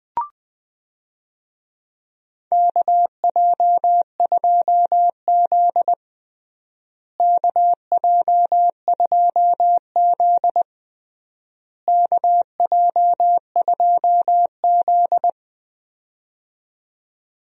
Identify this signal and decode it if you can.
Morse code, heard as a keyed tone, about 20 wpm, 700 Hz. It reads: KJ2Z KJ2Z KJ2Z
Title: USA Call Signs - 4 Characters - Repeated 3x - 20wpm